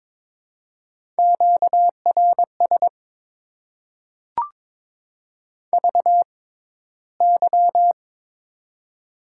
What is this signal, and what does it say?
Morse code, heard as a keyed tone, about 22 wpm, spaced as for 14 wpm, 700 Hz. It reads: QRS VY
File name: Top 90 QSO Elements - 6x Character Spacing - 22wpm